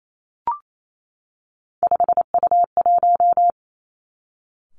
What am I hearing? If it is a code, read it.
Morse code, heard as a keyed tone, about 28 wpm, 700 Hz. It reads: 5U1
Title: Three Challenging Letter-Number Combinations 28wpm